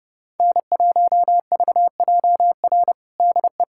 Morse code, heard as a keyed tone, about 30 wpm, 700 Hz. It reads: N1VJR DE
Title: USA Callsign DE Callsign - 5 Characters - 30wpm